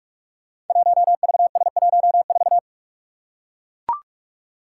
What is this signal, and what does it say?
Morse code, heard as a keyed tone, about 44 wpm, 700 Hz. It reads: 1VS14
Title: Five Challenging Letter-Number Combinations 45wpm